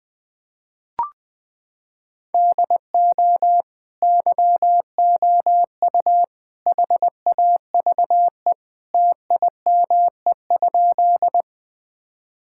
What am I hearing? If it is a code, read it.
Morse code, heard as a keyed tone, about 20 wpm, 700 Hz. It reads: DO YOU HAVE TIME?
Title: Sentences with 3-Word Phrases - 20wpm